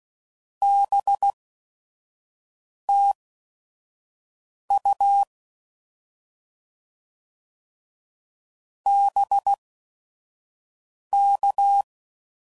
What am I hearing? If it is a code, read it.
Morse code, heard as a keyed tone, about 16 wpm, spaced as for 5 wpm, 800 Hz. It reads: BTU BK